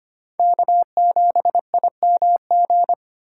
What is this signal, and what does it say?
Morse code, heard as a keyed tone, about 32 wpm, 700 Hz. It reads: K7IMG